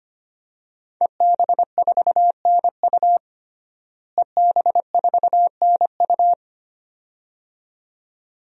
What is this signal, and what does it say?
Morse code, heard as a keyed tone, about 25 wpm, 700 Hz. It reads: EB4NU EB4NU
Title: DXCC-100 Call Signs - 5 Characters - Repeated 2x - 25wpm